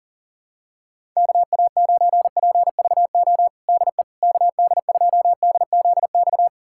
Morse code, heard as a keyed tone, about 40 wpm, 700 Hz. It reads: KA9PVO DE KD2DZX
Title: USA Callsign DE Callsign - 6 Characters - 40wpm